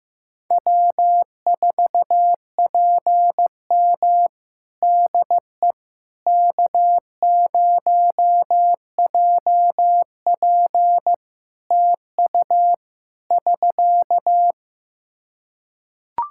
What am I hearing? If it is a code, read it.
Morse code, heard as a keyed tone, about 15 wpm, 700 Hz. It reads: W4PM DE K0JP TU <SK>
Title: Practice QSO - Exchange 4 - Ending - 15wpm